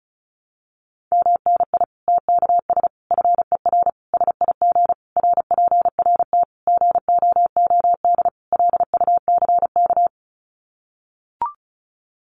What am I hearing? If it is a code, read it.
Morse code, heard as a keyed tone, about 34 wpm, 700 Hz. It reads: MNI TKS FER SIG RPRT GOOD LUCK
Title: Common QSO Phrases 35wpm